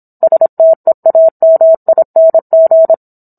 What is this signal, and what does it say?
Morse code, heard as a keyed tone, about 27 wpm, 650 Hz. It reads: STEAMING